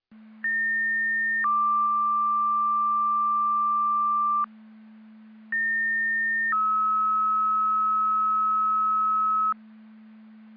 Two-tone page, heard as a one-second tone, then a three-second tone, about 1800 then 1180 Hz, about 1800 then 1290 Hz. It is heard from 0.4 to 9.5 s.
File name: Newton EMS Duty